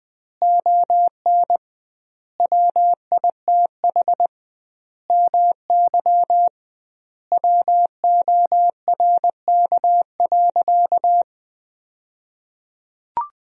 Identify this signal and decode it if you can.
Morse code, heard as a keyed tone, about 20 wpm, 700 Hz. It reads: ON WITH MY WORK.
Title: 2x Word-Spacing - Sentences from Top 300 Words - 20wpm